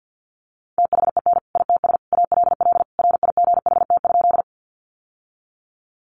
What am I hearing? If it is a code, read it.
Morse code, heard as a keyed tone, about 50 wpm, 700 Hz. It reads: THEN ITS ALL RIGHT?
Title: Sentences from Top 600 Words 50wpm